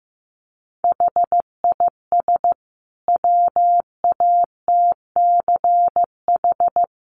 Morse code, heard as a keyed tone, about 15 wpm, 700 Hz. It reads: HIS WATCH
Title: Sentences from Top 1300 Words 15wpm